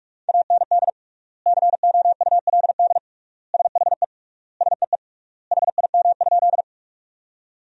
Morse code, heard as a keyed tone, about 46 wpm, 700 Hz. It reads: AND COULD SHE SEE HIM?
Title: 3x Word-Spacing - Sentences from Top 100 Words - 45wpm